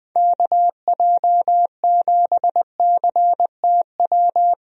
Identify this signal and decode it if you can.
Morse code, heard as a keyed tone, about 20 wpm, 700 Hz. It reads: KJ7CTW
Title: USA Call Signs - 6 Characters - Repeated 3x - 20wpm